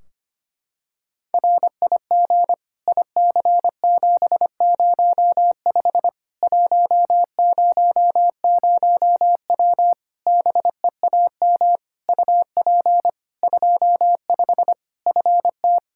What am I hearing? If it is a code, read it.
Morse code, heard as a keyed tone, about 25 wpm, 700 Hz. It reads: RIG IC705 100W BEAM UP 25 FT
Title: K1USN SST QSO Practice 25wpm